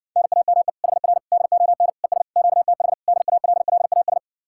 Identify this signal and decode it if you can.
Morse code, heard as a keyed tone, about 52 wpm, 700 Hz. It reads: NAME HR DON ES QTH DALLAS